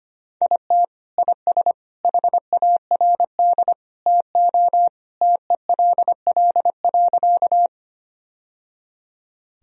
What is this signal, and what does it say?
Morse code, heard as a keyed tone, about 25 wpm, 700 Hz. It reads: IT IS HARD TO TELL.
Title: Sentences from Top 1000 Words - Review Part 1 - 25wpm